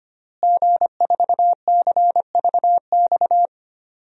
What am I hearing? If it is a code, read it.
Morse code, heard as a keyed tone, about 25 wpm, 700 Hz. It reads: G4CVX
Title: DXCC-100 Call Signs - 5 Characters - Repeated 2x - 25wpm